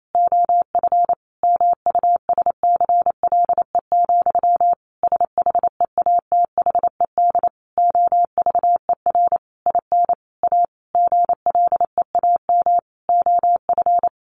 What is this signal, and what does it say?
Morse code, heard as a keyed tone, about 28 wpm, 700 Hz. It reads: OF MUSCLE, SHEATHED OVER IN A GLEAM OF
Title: A Sound Of Thunder 28wpm